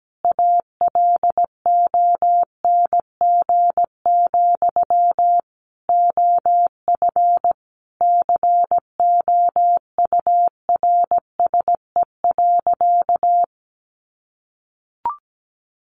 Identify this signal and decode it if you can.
Morse code, heard as a keyed tone, about 17 wpm, 700 Hz. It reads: ALONG, OF COURSE.
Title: Sentences from Top 700 Words 17wpm